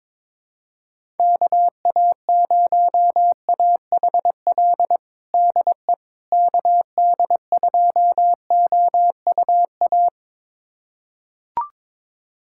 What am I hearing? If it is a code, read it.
Morse code, heard as a keyed tone, about 22 wpm, 700 Hz. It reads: KA0AHL DE KD2OUA
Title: USA Callsign DE Callsign 22wpm